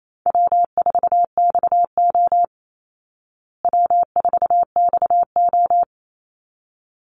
Morse code, heard as a keyed tone, about 28 wpm, 700 Hz. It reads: W4XO W4XO
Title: Easy Call Signs Repeated 2x - 28wpm